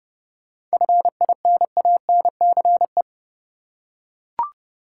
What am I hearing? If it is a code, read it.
Morse code, heard as a keyed tone, about 30 wpm, 700 Hz. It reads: FINANCE
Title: Seven Letter Words - 30wpm